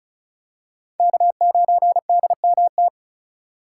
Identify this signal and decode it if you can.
Morse code, heard as a keyed tone, about 34 wpm, 700 Hz. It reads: K9DMT
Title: USA Call Signs - 5 Characters - 3x Character Spacing - 35wpm